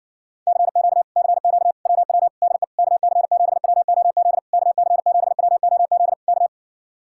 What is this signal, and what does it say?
Morse code, heard as a keyed tone, about 67 wpm, 700 Hz. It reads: CQ CQ WW DE KC7WOZ KC7WOZ K